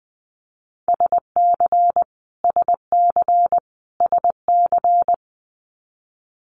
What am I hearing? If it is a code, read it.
Morse code, heard as a keyed tone, about 20 wpm, 700 Hz. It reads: SC SC SC
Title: US States Repeated 3x 20wpm